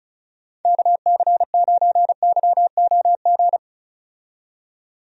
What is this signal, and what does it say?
Morse code, heard as a keyed tone, about 34 wpm, 700 Hz. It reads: KC9YOG
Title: USA Call Signs - 6 Characters - 35wpm